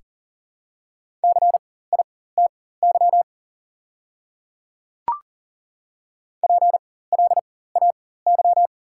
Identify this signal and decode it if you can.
Morse code, heard as a keyed tone, about 39 wpm, spaced as for 19 wpm, 690 Hz. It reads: CITY PLAY
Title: 4x Character Spacing - Top 300 Words - 40wpm